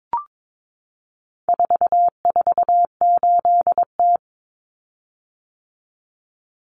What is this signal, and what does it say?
Morse code, heard as a keyed tone, about 22 wpm, 700 Hz. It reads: 448T
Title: Common Cut Numbers - 4-Digit 22wpm